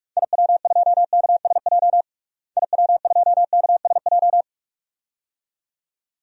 Morse code, heard as a keyed tone, about 48 wpm, 700 Hz. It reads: IW2KSJ IW2KSJ
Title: DXCC-100 Call Signs - 6 Characters - Repeated 2x - 45wpm